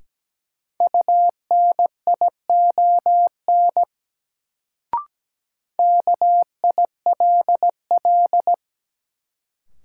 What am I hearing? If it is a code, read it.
Morse code, heard as a keyed tone, about 17 wpm, 700 Hz. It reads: UNION KILL
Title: Sentences from Top 1400 Words - New Word Drill Alt - 17wpm